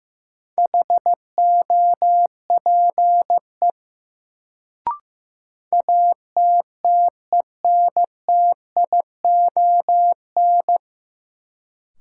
Morse code, heard as a keyed tone, about 15 wpm, 700 Hz. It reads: HOPE ATTENTION
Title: Sentences from Top 1000 Words - New Word Drill - 15wpm